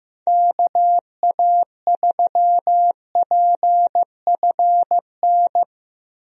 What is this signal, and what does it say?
Morse code, heard as a keyed tone, about 15 wpm, 700 Hz. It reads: KA3PFN
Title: USA Call Signs - 6 Characters - 15wpm